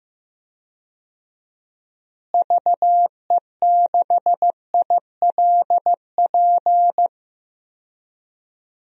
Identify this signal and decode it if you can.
Morse code, heard as a keyed tone, about 15 wpm, 700 Hz. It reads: VE6ILP
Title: DXCC-100 Call Signs - 6 Characters - Repeated 2x - 15wpm